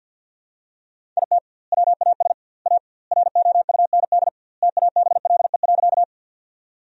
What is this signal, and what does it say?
Morse code, heard as a keyed tone, about 50 wpm, 700 Hz. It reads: IT WAS A ROUND TABLE.
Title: 2x Word-Spacing - Sentences from Top 400 Words - 50wpm